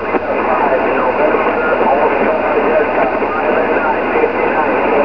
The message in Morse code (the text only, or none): none